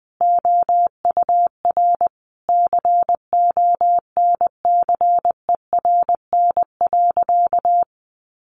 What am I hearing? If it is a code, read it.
Morse code, heard as a keyed tone, about 20 wpm, 700 Hz. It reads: OUR CONCERN.
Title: Sentences from Top 1400 Words 20wpm